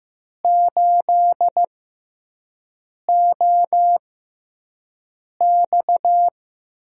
Morse code, heard as a keyed tone, about 15 wpm, spaced as for 5 wpm, 700 Hz. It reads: 8OX
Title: USA Call Signs - 6x Character Spacing - 15wpm